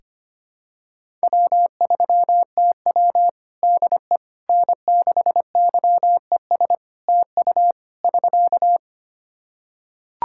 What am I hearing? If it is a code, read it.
Morse code, heard as a keyed tone, about 25 wpm, 700 Hz. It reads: W3TW DE N6YES TU <SK>
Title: Practice QSO - Exchange 4 - Ending - 25wpm